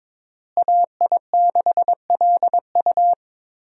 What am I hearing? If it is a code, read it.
Morse code, heard as a keyed tone, about 22 wpm, 700 Hz. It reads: AI6LU
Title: USA Call Signs - 5 Characters - Repeated 3x - 22wpm